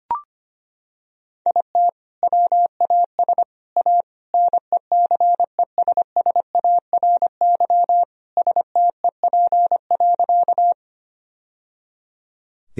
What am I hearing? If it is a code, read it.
Morse code, heard as a keyed tone, about 25 wpm, 700 Hz. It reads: IT WAS A NECESSARY STEP.
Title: Sentences from Top 1000 Words - Review Part 2 - 25wpm